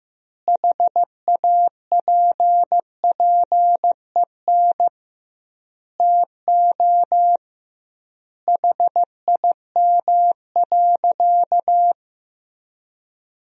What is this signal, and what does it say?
Morse code, heard as a keyed tone, about 15 wpm, 700 Hz. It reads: HAPPEN TO HIM.
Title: 2x Word-Spacing - Sentences from Top 400 Words - 15wpm